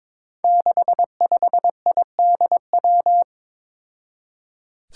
Morse code, heard as a keyed tone, about 21 wpm, 700 Hz. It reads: 65IDW
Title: Five Challenging Letter-Number Combinations 22wpm